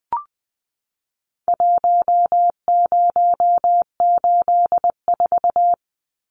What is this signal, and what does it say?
Morse code, heard as a keyed tone, about 20 wpm, 700 Hz. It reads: 1084
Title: Numbers - Contest Style - 4-Digit 20wpm